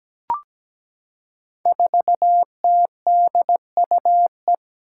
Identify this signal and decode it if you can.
Morse code, heard as a keyed tone, about 17 wpm, 700 Hz. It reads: 4TDUE
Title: Five Challenging Letter-Number Combinations 17wpm